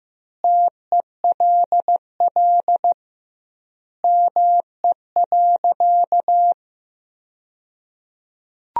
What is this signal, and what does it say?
Morse code, heard as a keyed tone, about 16 wpm, 700 Hz. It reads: TELL ME.
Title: 2x Word-Spacing - Sentences from Top 200 Words - 15wpm